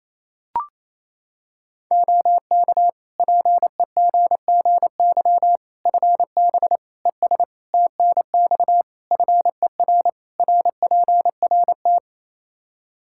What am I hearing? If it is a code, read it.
Morse code, heard as a keyed tone, about 27 wpm, 700 Hz. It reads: OK PEGGY FB ES TNX FER RPRT
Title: Practice QSO - Exchange 2 - Info - 28wpm